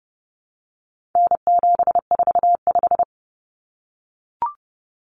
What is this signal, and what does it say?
Morse code, heard as a keyed tone, about 30 wpm, 700 Hz. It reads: N745
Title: Common Cut Numbers - 4-Digit 30wpm